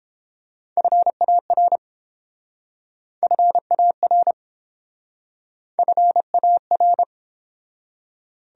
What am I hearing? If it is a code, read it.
Morse code, heard as a keyed tone, about 30 wpm, 700 Hz. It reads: FAR FAR FAR